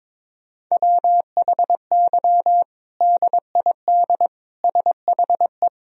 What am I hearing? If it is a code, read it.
Morse code, heard as a keyed tone, about 22 wpm, 700 Hz. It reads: WHY DID SHE
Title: Sentences from Top 1000 Words - Review Part 3 - 22wpm